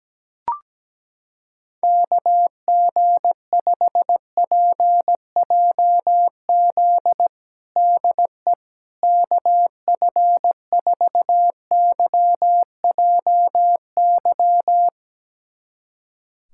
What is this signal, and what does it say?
Morse code, heard as a keyed tone, about 17 wpm, 700 Hz. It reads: KG5PJZ DE KF4YJY
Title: USA Callsign DE Callsign - 6 Characters - 17wpm